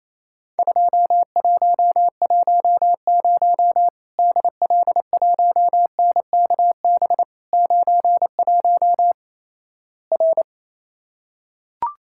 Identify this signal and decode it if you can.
Morse code, heard as a keyed tone, about 28 wpm, 700 Hz. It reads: 2110 DL1NKB 91 R
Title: Worked All Europe DX Contest - QTC Exchange - 28wpm